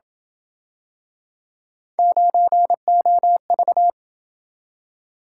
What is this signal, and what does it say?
Morse code, heard as a keyed tone, about 27 wpm, 700 Hz. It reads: 9OV